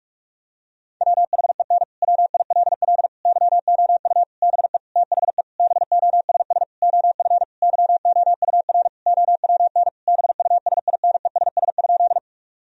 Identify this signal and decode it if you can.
Morse code, heard as a keyed tone, about 46 wpm, 700 Hz. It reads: WHEN WILL YOU BE THE BOSS OF YOUR OWN BUSINESS?
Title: Sentences from Top 1000 Words - Review Part 4 - 45wpm